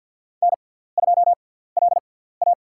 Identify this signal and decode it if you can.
Morse code, heard as a keyed tone, about 47 wpm, spaced as for 17 wpm, 700 Hz. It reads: N2LA
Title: USA Call Signs - 4 Characters - 6x Character Spacing - 50wpm